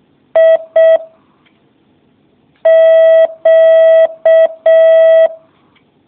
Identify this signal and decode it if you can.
Morse code, heard as a keyed tone, about 6 wpm, 650 Hz. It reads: I Q